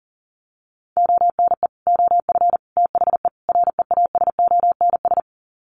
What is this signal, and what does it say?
Morse code, heard as a keyed tone, about 39 wpm, 700 Hz. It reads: ONE OF THE REASONS
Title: Sets of 4 Words 40wpm